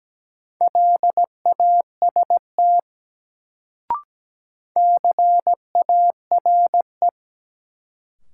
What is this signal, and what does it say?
Morse code, heard as a keyed tone, about 17 wpm, 700 Hz. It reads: LAST CARE
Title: Sentences from Top 300 Words - New Word Drill - 17wpm